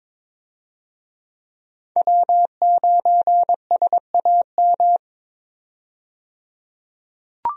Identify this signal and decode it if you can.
Morse code, heard as a keyed tone, about 22 wpm, 700 Hz. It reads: W9SAM
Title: USA Call Signs - 5 Characters - 22wpm